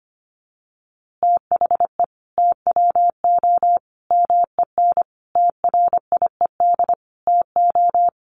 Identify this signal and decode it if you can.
Morse code, heard as a keyed tone, about 25 wpm, 700 Hz. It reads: THE TWO MEN TRIED TO
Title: A Sound Of Thunder 25wpm